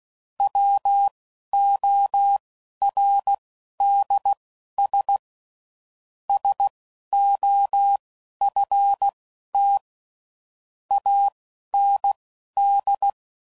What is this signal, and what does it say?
Morse code, heard as a keyed tone, about 16 wpm, spaced as for 11 wpm, 800 Hz. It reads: WORDS SOFT AND